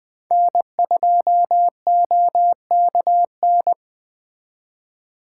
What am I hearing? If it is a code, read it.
Morse code, heard as a keyed tone, about 21 wpm, 700 Hz. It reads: N2OKN
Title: Speed Racing - USA 1x3 Call Signs 17wpm